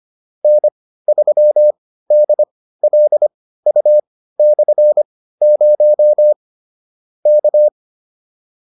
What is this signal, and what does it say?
Morse code, heard as a keyed tone, about 25 wpm, spaced as for 15 wpm, 600 Hz. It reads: N3DLU/0 K